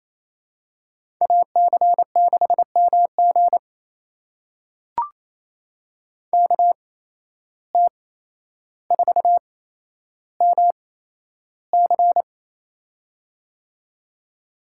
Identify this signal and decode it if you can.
Morse code, heard as a keyed tone, about 28 wpm, spaced as for 12 wpm, 700 Hz. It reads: AC6MG KT4MC
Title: USA Call Signs - 5 Characters - 8x Character Spacing - 28wpm